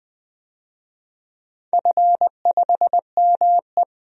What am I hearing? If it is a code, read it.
Morse code, heard as a keyed tone, about 20 wpm, 700 Hz. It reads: F5ME
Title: DXCC-100 Call Signs - 4 Characters - 20wpm